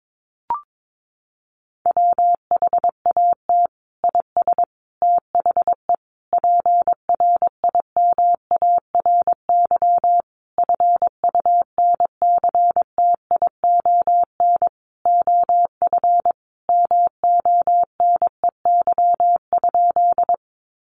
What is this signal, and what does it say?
Morse code, heard as a keyed tone, about 22 wpm, 700 Hz. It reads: WHAT IS THE PRIMARY FUNCTION OF MONEY?
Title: Sentences from Top 1100 Words 22wpm